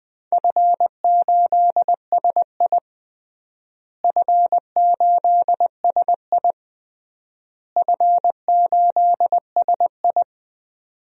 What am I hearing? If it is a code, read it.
Morse code, heard as a keyed tone, about 20 wpm, 700 Hz. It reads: F8SI F8SI F8SI
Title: DXCC-100 Call Signs - 4 Characters - Repeated 3x - 20wpm